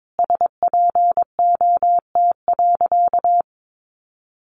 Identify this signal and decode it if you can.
Morse code, heard as a keyed tone, about 22 wpm, 700 Hz. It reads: SPOT.
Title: Sentences from Top 1200 Words 22wpm